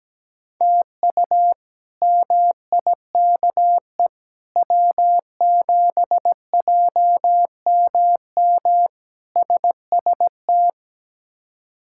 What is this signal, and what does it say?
Morse code, heard as a keyed tone, about 17 wpm, 700 Hz. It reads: TU MIKE W7JMM SST
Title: K1USN SST QSO Practice - 17wpm